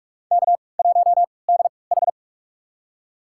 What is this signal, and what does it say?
Morse code, heard as a keyed tone, about 46 wpm, spaced as for 25 wpm, 700 Hz. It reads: K1DH
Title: USA Call Signs - 4 Characters - 3x Character Spacing - 45wpm